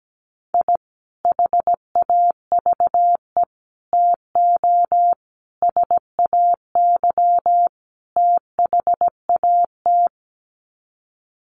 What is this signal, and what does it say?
Morse code, heard as a keyed tone, about 17 wpm, 700 Hz. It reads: I HAVE TO SAY THAT
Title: Sets of 5 Words Encore 17wpm